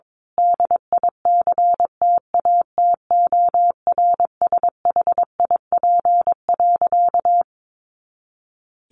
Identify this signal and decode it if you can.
Morse code, heard as a keyed tone, about 22 wpm, 700 Hz. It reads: DICTATORSHIP.
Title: A Sound Of Thunder 22wpm